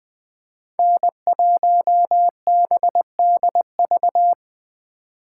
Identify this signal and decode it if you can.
Morse code, heard as a keyed tone, about 20 wpm, 700 Hz. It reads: N1BDV
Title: Speed Racing - USA 1x3 Call Signs 17wpm